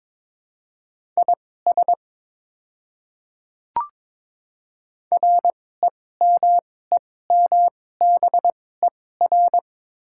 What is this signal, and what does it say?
Morse code, heard as a keyed tone, about 21 wpm, spaced as for 16 wpm, 690 Hz. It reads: IS REMEMBER
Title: 2x Character Spacing - Top 500 Words - 22wpm